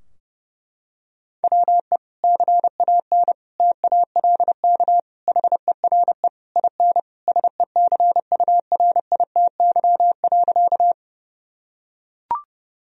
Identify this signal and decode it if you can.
Morse code, heard as a keyed tone, about 30 wpm, 700 Hz. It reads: WE CAN TALK HERE IN SECURITY.
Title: Sentences from Top 1000 Words - Review Part 4 - 30wpm